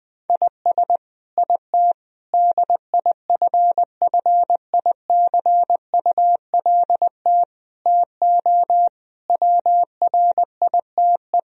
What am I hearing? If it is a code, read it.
Morse code, heard as a keyed tone, about 20 wpm, 700 Hz. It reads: IS IT DIFFICULT TO WRITE
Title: Sentences from Top 1000 Words - Review Part 3 - 20wpm